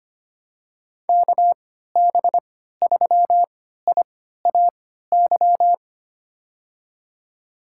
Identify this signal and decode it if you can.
Morse code, heard as a keyed tone, about 25 wpm, spaced as for 14 wpm, 700 Hz. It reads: KB3IAY
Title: USA Call Signs - 3x Character Spacing - 25wpm